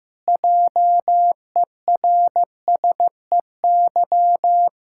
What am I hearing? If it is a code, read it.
Morse code, heard as a keyed tone, about 16 wpm, 700 Hz. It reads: JERSEY
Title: Practice QSO - Exchange 1 - Beginning - 15wpm